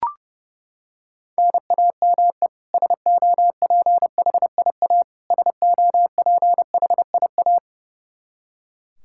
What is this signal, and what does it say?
Morse code, heard as a keyed tone, about 30 wpm, 700 Hz. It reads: NAME SOPHIA SOPHIA
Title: Practice QSO - Exchange 1 - Beginning - 30wpm